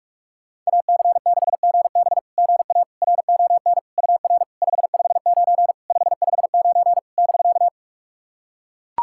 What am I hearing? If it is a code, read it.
Morse code, heard as a keyed tone, about 44 wpm, 690 Hz. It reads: AC6GB GA RON UR 559 559 <BK>
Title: SOTA QSO Practice 45wpm